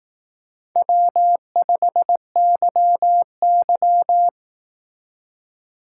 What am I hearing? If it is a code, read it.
Morse code, heard as a keyed tone, about 18 wpm, 700 Hz. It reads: W5YY
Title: Speed Racing - USA 1x2 Call Signs 15wpm